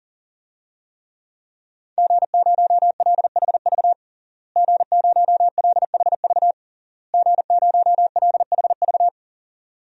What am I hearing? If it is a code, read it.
Morse code, heard as a keyed tone, about 37 wpm, 700 Hz. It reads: G0LHV G0LHV G0LHV
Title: DXCC-100 Call Signs - 5 Characters - Repeated 3x - 40wpm